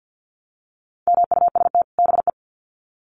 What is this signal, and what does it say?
Morse code, heard as a keyed tone, about 50 wpm, 700 Hz. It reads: MUST BE
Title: Sets of 2 Words 50wpm